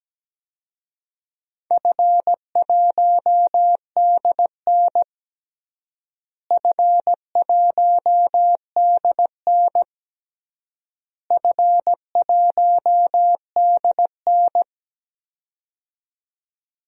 Morse code, heard as a keyed tone, about 17 wpm, 700 Hz. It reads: F1DN F1DN F1DN
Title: DXCC-100 Call Signs - 4 Characters - Repeated 3x - 17wpm